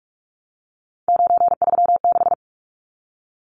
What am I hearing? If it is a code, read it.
Morse code, heard as a keyed tone, about 44 wpm, 700 Hz. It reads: 936
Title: Numbers - 3-Digit 45wpm